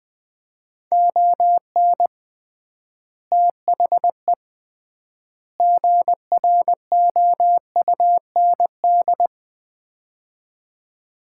Morse code, heard as a keyed tone, about 20 wpm, 700 Hz. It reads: ON THE GROUND
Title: 3x Word-Spacing - Sets of 3 Words - 20wpm